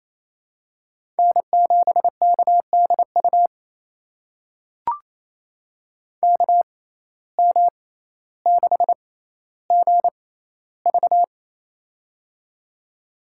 Morse code, heard as a keyed tone, about 28 wpm, spaced as for 14 wpm, 700 Hz. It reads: N7KDU KM6GV
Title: USA Call Signs - 5 Characters - 6x Character Spacing - 28wpm